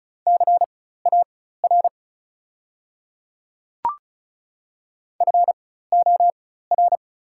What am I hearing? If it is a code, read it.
Morse code, heard as a keyed tone, about 36 wpm, spaced as for 16 wpm, 690 Hz. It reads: CAR FOR
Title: 4x Character Spacing - Top 300 Words - 35wpm